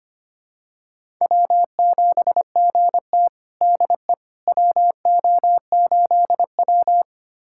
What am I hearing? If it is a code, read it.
Morse code, heard as a keyed tone, about 25 wpm, 700 Hz. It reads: W7GT DE WO8W
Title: USA Callsign DE Callsign - 4 Characters - 25wpm